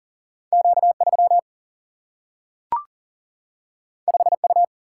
Morse code, heard as a keyed tone, about 39 wpm, 700 Hz. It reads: Q 3 5 U
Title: Single-Letter Single-Number 40wpm